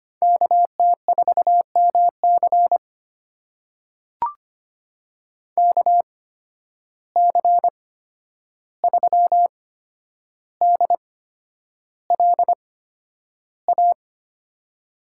Morse code, heard as a keyed tone, about 25 wpm, spaced as for 10 wpm, 700 Hz. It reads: KT4MC KC3DLA
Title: USA Call Signs - 8x Character Spacing - 25wpm